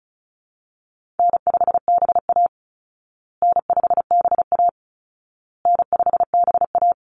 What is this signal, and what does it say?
Morse code, heard as a keyed tone, about 34 wpm, 700 Hz. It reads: N5BA N5BA N5BA
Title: Easy Call Signs Repeated 3x - 35wpm